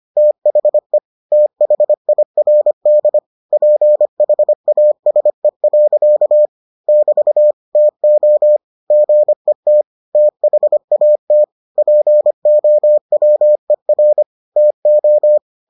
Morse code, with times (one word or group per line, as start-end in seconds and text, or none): 0.2-1.0 THE
1.3-3.2 THIRD
3.5-6.5 PHASE.
6.9-7.5 <BT>
7.8-8.6 TO
8.9-9.8 GET
10.2-11.4 THAT
11.8-14.2 POWER
14.6-15.4 TO